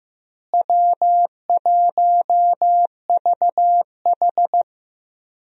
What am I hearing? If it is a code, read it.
Morse code, heard as a keyed tone, about 15 wpm, 700 Hz. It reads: W1VH
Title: USA Call Signs - 4 Characters - Repeated 3x - 15wpm